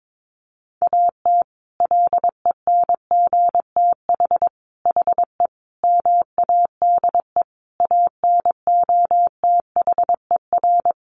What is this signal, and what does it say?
Morse code, heard as a keyed tone, about 22 wpm, 700 Hz. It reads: AT LENGTH HE MADE ANOTHER
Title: Sentences from Top 1000 Words 22wpm